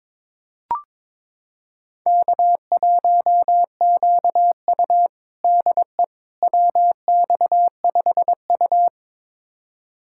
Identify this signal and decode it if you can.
Morse code, heard as a keyed tone, about 22 wpm, 700 Hz. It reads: K1QU DE WX5U
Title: USA Callsign DE Callsign - 4 Characters - 22wpm